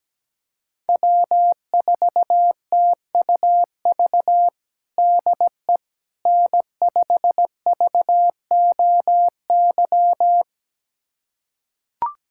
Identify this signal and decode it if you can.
Morse code, heard as a keyed tone, about 17 wpm, 700 Hz. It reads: W4TUV DE N5VOY
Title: USA Callsign DE Callsign - 5 Characters - 17wpm